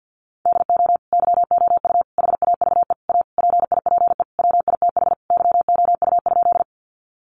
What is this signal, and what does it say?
Morse code, heard as a keyed tone, about 50 wpm, 700 Hz. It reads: DO YOU HAVE A PIPE WITH YOU?
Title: Sentences from Top 1500 Words 50wpm